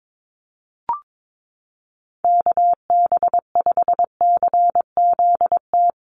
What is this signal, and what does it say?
Morse code, heard as a keyed tone, about 22 wpm, 700 Hz. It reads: KB5CZT
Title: USA Call Signs Repeated 2x 22wpm